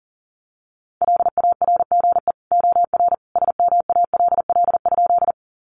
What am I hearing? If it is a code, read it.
Morse code, heard as a keyed tone, about 40 wpm, 700 Hz. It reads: LARGE OR SMALL?
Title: Sentences from Top 200 Words 40wpm